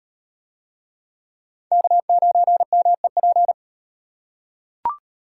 Morse code, heard as a keyed tone, about 36 wpm, 700 Hz. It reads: K9MEP